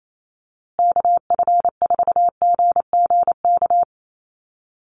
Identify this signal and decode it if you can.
Morse code, heard as a keyed tone, about 28 wpm, 700 Hz. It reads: KF4GGK
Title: USA Call Signs Repeated 2x 28wpm